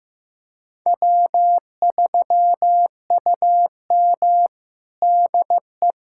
Morse code, heard as a keyed tone, about 16 wpm, 700 Hz. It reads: W3UM DE
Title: USA Callsign DE Callsign - 4 Characters - 15wpm